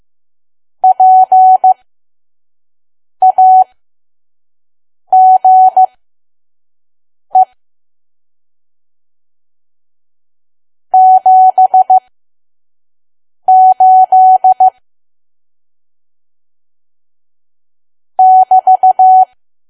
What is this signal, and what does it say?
Morse code, heard as a keyed tone, about 15 wpm, spaced as for 5 wpm, 750 Hz. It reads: PAGE 78 <BT>
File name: Archive / 5 WPM files / PM